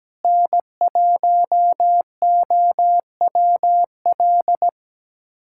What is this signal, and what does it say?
Morse code, heard as a keyed tone, about 17 wpm, 700 Hz. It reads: N1OWL
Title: USA Callsign DE Callsign - 5 Characters - 17wpm